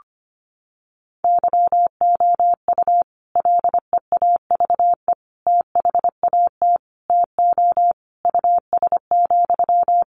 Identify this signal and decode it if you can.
Morse code, heard as a keyed tone, about 25 wpm, 700 Hz. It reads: YOU LEAVE THAT TO US,
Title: Sentences from Top 1400 Words 25wpm